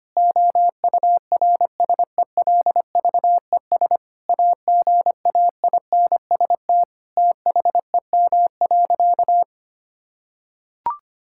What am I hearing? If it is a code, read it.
Morse code, heard as a keyed tone, about 25 wpm, 700 Hz. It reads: OURSELVES AGAINST THEM.
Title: Sentences from Top 1600 Words 25wpm